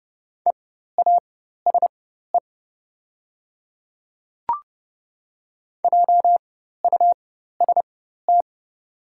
Morse code, heard as a keyed tone, about 30 wpm, spaced as for 14 wpm, 700 Hz. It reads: EASE JUST
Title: 4x Character Spacing - Top 300 Words - 30wpm